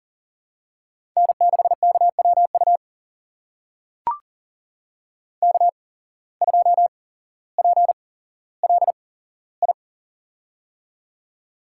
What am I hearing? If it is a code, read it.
Morse code, heard as a keyed tone, about 40 wpm, spaced as for 17 wpm, 700 Hz. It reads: N6KWU K2PLI
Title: USA Call Signs - 5 Characters - 8x Character Spacing - 40wpm